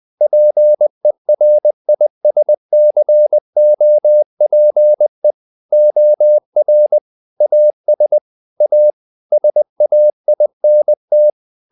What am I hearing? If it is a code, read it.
Morse code, heard as a keyed tone, about 20 wpm, 600 Hz. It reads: PERISCOPE OR AS A SAINT
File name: the old man and the sea part 2 20wpm